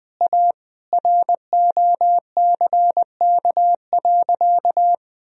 Morse code, heard as a keyed tone, about 21 wpm, 700 Hz. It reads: A ROCK.
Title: Sentences from Top 1000 Words - Review Part 2 - 20wpm